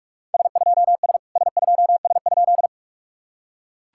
Morse code, heard as a keyed tone, about 44 wpm, 660 Hz. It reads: S2S S2S?
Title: SOTA QSO Practice 45wpm